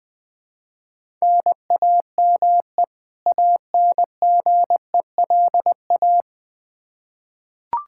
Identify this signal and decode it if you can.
Morse code, heard as a keyed tone, about 20 wpm, 700 Hz. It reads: NAME ANGELA
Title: Practice QSO - Exchange 1 - Beginning - 20wpm